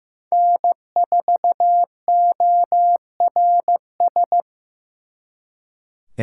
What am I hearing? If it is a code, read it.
Morse code, heard as a keyed tone, about 16 wpm, 700 Hz. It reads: N4ORS
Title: Speed Racing - USA 1x3 Call Signs 15wpm